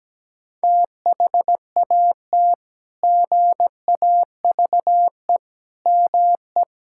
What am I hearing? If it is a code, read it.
Morse code, heard as a keyed tone, about 17 wpm, 700 Hz. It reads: THAT GAVE ME
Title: Sentences from Top 1600 Words 17wpm